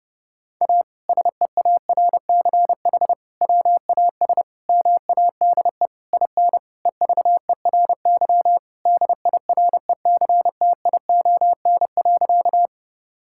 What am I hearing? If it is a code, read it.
Morse code, heard as a keyed tone, about 28 wpm, 700 Hz. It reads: A SEARCH WAS MADE IN EVERY DIRECTION.
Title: Sentences from Top 1000 Words - Review Part 5 - 30wpm